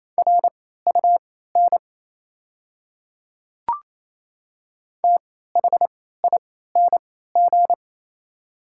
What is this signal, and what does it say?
Morse code, heard as a keyed tone, about 27 wpm, spaced as for 16 wpm, 700 Hz. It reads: RUN THING
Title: 3x Character Spacing - Top 300 Words - 28wpm